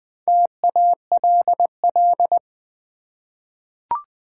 Morse code, heard as a keyed tone, about 20 wpm, 700 Hz. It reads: TALL
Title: Four Letter Words - 20wpm